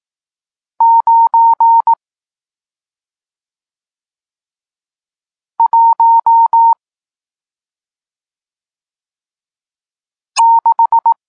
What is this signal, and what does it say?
Morse code, heard as a keyed tone, about 18 wpm, 920 Hz. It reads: 9 1 6
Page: Tekst 40 – Nauka znaków przestankowych